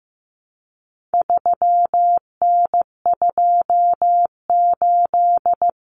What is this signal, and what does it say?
Morse code, heard as a keyed tone, about 15 wpm, 700 Hz. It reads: 3N28
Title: Common Cut Numbers - 4-Digit 15wpm